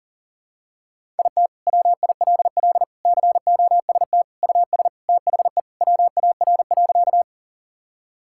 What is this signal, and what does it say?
Morse code, heard as a keyed tone, about 37 wpm, 700 Hz. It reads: IT WILL COST US THE WAR.
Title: Sentences from Top 1000 Words - Review Part 5 - 40wpm